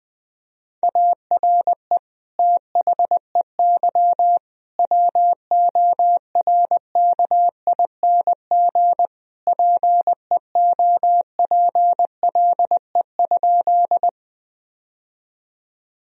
Morse code, heard as a keyed tone, about 20 wpm, 700 Hz. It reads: ARE THEY WORKING PEOPLE?
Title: Sentences from Top 1000 Words - Review Part 4 - 20wpm